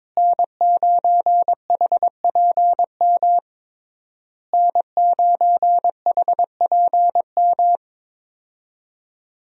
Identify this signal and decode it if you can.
Morse code, heard as a keyed tone, about 22 wpm, 700 Hz. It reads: N9HPM N9HPM
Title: USA Call Signs - 5 Characters - Repeated 3x - 22wpm